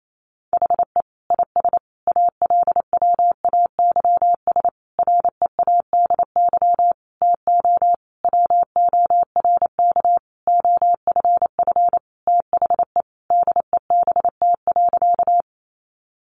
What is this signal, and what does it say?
Morse code, heard as a keyed tone, about 28 wpm, 700 Hz. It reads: HE IS ALWAYS READY TO WORK OFF THE DEBT.
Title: Sentences from Top 1300 Words 28wpm